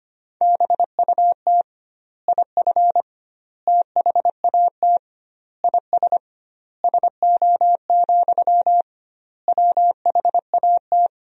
Morse code, heard as a keyed tone, about 25 wpm, 700 Hz. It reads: BUT IF THAT IS SO, WHAT
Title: 2x Word-Spacing - Sentences from Top 200 Words - 25wpm